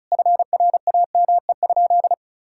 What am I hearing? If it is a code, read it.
Morse code, heard as a keyed tone, about 34 wpm, 700 Hz. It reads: FRAME?
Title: Sentences from Top 1000 Words - Review Part 6 - 35wpm